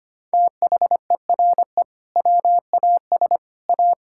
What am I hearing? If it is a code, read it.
Morse code, heard as a keyed tone, about 25 wpm, 700 Hz. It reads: THERE WAS A
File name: practice / 25wpm / Sentences with 2-Word Phrases - 25wpm